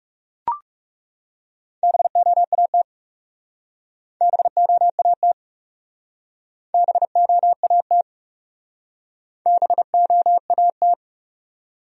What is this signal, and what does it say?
Morse code, heard as a keyed tone, about 36 wpm, 700 Hz. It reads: BOAT BOAT BOAT BOAT
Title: Speed-Racing Top 500 Words 30wpm